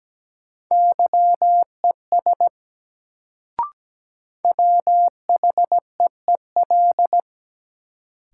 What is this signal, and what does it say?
Morse code, heard as a keyed tone, about 17 wpm, 700 Hz. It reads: YES WHEEL
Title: Sentences from Top 500 Words - New Word Drill - 17wpm